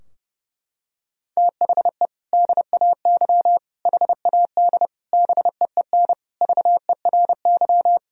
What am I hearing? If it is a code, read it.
Morse code, heard as a keyed tone, about 30 wpm, 700 Hz. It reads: THE DAY HAD BEEN VERY
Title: Sentences from Top 1000 Words - Review Part 3 - 30wpm